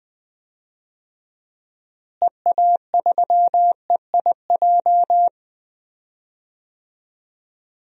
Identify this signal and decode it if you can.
Morse code, heard as a keyed tone, about 20 wpm, 700 Hz. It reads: EA3EIJ
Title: DXCC-100 Call Signs - 6 Characters - Repeated 3x - 20wpm